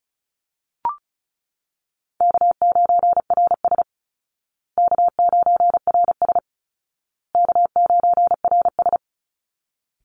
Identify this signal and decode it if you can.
Morse code, heard as a keyed tone, about 34 wpm, 700 Hz. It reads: K9RS K9RS K9RS
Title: Easy Call Signs Repeated 3x - 35wpm